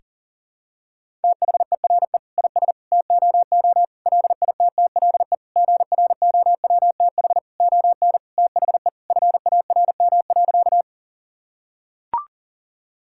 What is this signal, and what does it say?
Morse code, heard as a keyed tone, about 37 wpm, 700 Hz. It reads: THERE IS TOO LITTLE GROWTH ON THE FARM.
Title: Sentences from Top 1000 Words - Review Part 5 - 40wpm